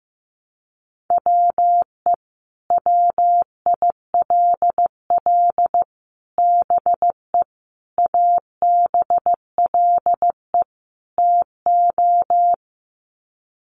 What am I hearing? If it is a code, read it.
Morse code, heard as a keyed tone, about 15 wpm, 700 Hz. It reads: WE WILL BE ABLE TO
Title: Sets of 5 Words Encore 15wpm